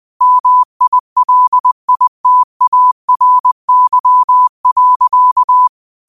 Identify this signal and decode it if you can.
Morse code, heard as a keyed tone, about 20 wpm, 1000 Hz. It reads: MILITARY.
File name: qotdfiles / 20wpm